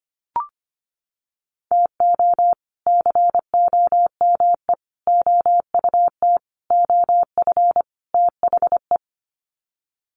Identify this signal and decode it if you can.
Morse code, heard as a keyed tone, about 25 wpm, 700 Hz. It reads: TO COME OUT OF THE
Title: Sets of 5 Words Encore2 25wpm